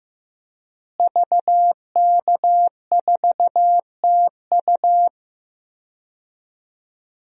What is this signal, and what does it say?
Morse code, heard as a keyed tone, about 15 wpm, 700 Hz. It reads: VK4TU
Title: DXCC-100 Call Signs - 5 Characters - Repeated 3x - 15wpm